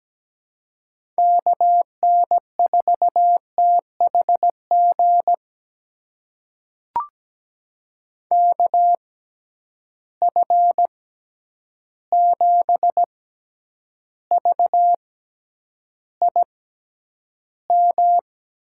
Morse code, heard as a keyed tone, about 17 wpm, spaced as for 9 wpm, 700 Hz. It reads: KN4THG KF7VIM
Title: USA Call Signs - 6x Character Spacing - 17wpm